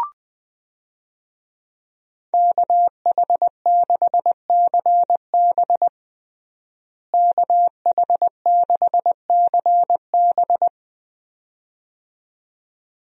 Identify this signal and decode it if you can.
Morse code, heard as a keyed tone, about 20 wpm, 700 Hz. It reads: KH6CB KH6CB
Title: USA Call Signs - 5 Characters - Repeated 2x - 20wpm